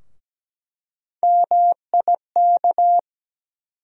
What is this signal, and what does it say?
Morse code, heard as a keyed tone, about 17 wpm, 700 Hz. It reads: MIK
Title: USA Call Sign Postfixes - 3 Letters 17wpm